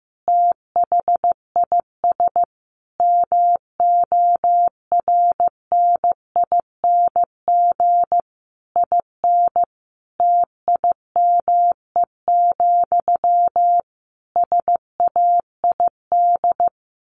Morse code, heard as a keyed tone, about 15 wpm, 700 Hz. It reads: THIS MORNING IN TIME, SAID
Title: A Sound Of Thunder 15wpm